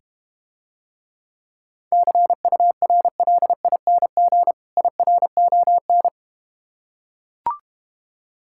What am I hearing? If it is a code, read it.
Morse code, heard as a keyed tone, about 32 wpm, 700 Hz. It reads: CURLING IRON